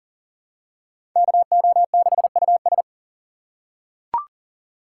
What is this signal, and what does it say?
Morse code, heard as a keyed tone, about 40 wpm, 700 Hz. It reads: KO6US
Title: USA Call Signs - 2x Character Spacing - 40wpm